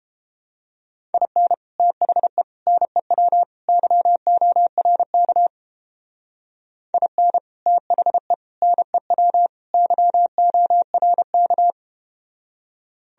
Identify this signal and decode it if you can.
Morse code, heard as a keyed tone, about 32 wpm, 700 Hz. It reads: IN THE NEW YORK IN THE NEW YORK